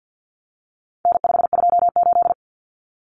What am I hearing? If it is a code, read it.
Morse code, heard as a keyed tone, about 50 wpm, 700 Hz. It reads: N528
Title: Common Cut Numbers - 4-Digit 50wpm